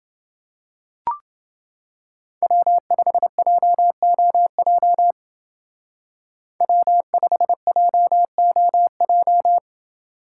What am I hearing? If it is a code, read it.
Morse code, heard as a keyed tone, about 28 wpm, 700 Hz. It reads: W5JOJ W5JOJ